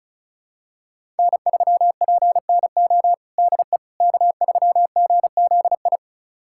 Morse code, heard as a keyed tone, about 36 wpm, 690 Hz. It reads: N3PNO DE K3GZI
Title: USA Callsign DE Callsign - 5 Characters - 35wpm